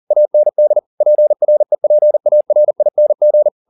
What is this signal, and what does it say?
Morse code, heard as a keyed tone, about 40 wpm, 600 Hz. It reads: AND PREPARING